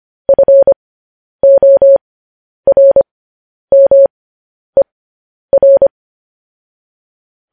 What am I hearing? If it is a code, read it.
Morse code, heard as a keyed tone, about 25 wpm, spaced as for 10 wpm, 550 Hz. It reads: FORMER